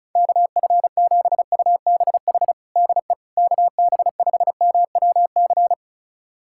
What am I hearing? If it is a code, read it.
Morse code, heard as a keyed tone, about 36 wpm, 700 Hz. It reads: KF7UBH DE KB5MWC
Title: USA Callsign DE Callsign - 6 Characters - 35wpm